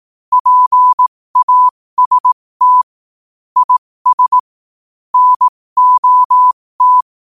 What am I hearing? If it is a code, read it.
Morse code, heard as a keyed tone, about 18 wpm, spaced as for 15 wpm, 1000 Hz. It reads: PAST IS NOT
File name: qotdfiles / 15wpm